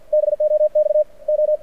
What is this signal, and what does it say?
Morse code, heard as a keyed tone, about 50 wpm, 600 Hz. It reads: BOX O
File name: archived / CRSnet